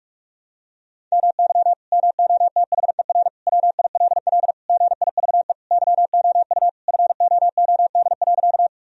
Morse code, heard as a keyed tone, about 46 wpm, 700 Hz. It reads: MY MOTHER WILL GIVE YOU FOOD.
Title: Sentences from Top 1000 Words - Review Part 1 - 45wpm